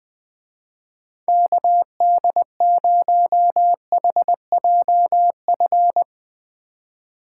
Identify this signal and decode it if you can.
Morse code, heard as a keyed tone, about 20 wpm, 700 Hz. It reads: KD0HJF